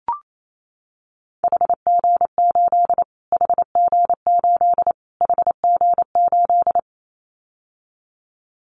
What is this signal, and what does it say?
Morse code, heard as a keyed tone, about 28 wpm, 700 Hz. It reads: HG8 HG8 HG8
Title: Callsign Prefixes for Top 25 DXCC - Repeated 3x - 28wpm